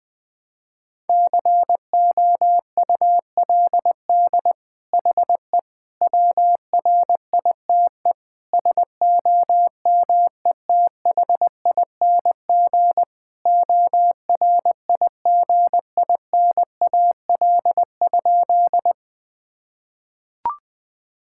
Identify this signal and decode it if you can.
Morse code, heard as a keyed tone, about 20 wpm, 700 Hz. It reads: COULD HE WRITE SOMETHING ORIGINAL?
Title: Sentences from Top 1000 Words - Review Part 6 - 20wpm